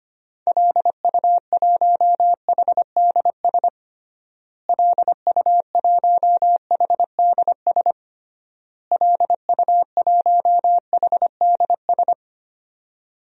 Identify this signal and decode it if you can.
Morse code, heard as a keyed tone, about 25 wpm, 690 Hz. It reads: LU1HDS LU1HDS LU1HDS
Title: DXCC-100 Call Signs - 6 Characters - Repeated 3x - 25wpm